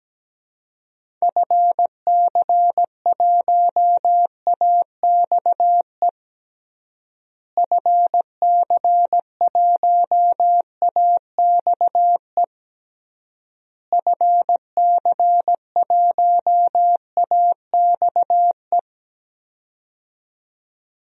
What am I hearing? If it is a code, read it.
Morse code, heard as a keyed tone, about 17 wpm, 700 Hz. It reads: FC1AXE FC1AXE FC1AXE
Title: DXCC-100 Call Signs - 6 Characters - Repeated 3x - 17wpm